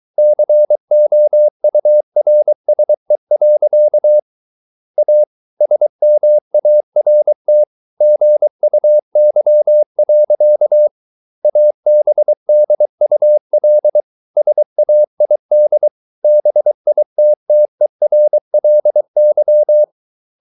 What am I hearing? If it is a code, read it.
Morse code, heard as a keyed tone, about 23 wpm, 600 Hz. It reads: COURSE. A SMART GUY. ABDUL SAID BITTERLY